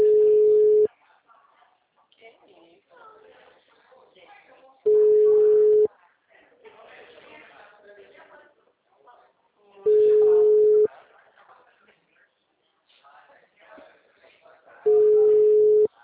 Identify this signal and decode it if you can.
Ringback tone (Europe, 425 Hz), with speech.